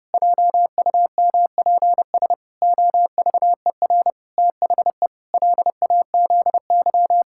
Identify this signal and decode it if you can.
Morse code, heard as a keyed tone, about 30 wpm, 700 Hz. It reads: JUMPS OVER THE LAZY
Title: Pangram Challenge v2 - 30wpm